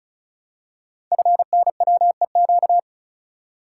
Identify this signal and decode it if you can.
Morse code, heard as a keyed tone, about 36 wpm, 700 Hz. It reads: FNWEQ
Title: Code Groups - 5 Letters - 35wpm